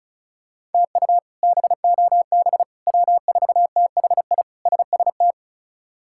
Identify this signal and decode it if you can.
Morse code, heard as a keyed tone, about 36 wpm, 700 Hz. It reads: TU BOB W4THI SST
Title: K1USN SST QSO Practice - 35wpm